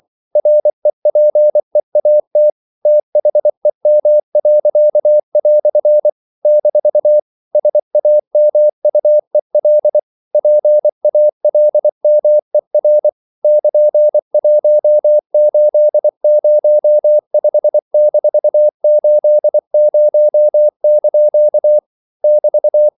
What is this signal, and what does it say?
Morse code, heard as a keyed tone, about 24 wpm, 600 Hz. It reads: REPEAT THEM." - SAMUEL PALMER <KN>1805-80) <BT>
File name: CW / Practice / EnglishQuotations0224